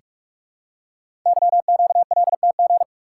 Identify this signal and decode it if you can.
Morse code, heard as a keyed tone, about 44 wpm, 700 Hz. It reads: YQLTG
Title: Code Groups - 5 Letters - 45wpm